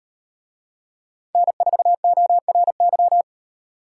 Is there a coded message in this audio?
Yes: Morse code from 1.4 to 3.2 s, heard as a keyed tone, about 38 wpm, 700 Hz. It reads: N4ORY